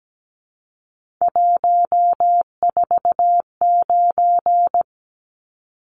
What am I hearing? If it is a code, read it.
Morse code, heard as a keyed tone, about 17 wpm, 700 Hz. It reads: 149
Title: Numbers - 3-Digit 17wpm